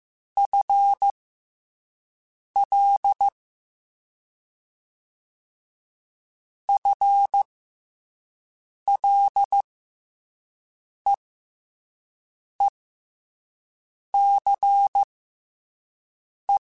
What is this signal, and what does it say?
Morse code, heard as a keyed tone, about 15 wpm, spaced as for 5 wpm, 800 Hz. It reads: FL FLEECE
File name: FL / FL mp3s / FL Prefixes 3x2 @5